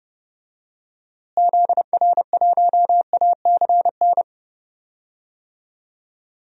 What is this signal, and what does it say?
Morse code, heard as a keyed tone, about 30 wpm, 700 Hz. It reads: ZR1ACN
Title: DXCC-100 Call Signs - 6 Characters - Repeated 3x - 30wpm